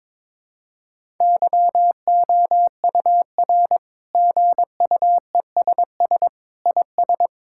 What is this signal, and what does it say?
Morse code, heard as a keyed tone, about 22 wpm, 700 Hz. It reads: YOUR GUESS IS
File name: Sentences with 7-Word Phrases - 22wpm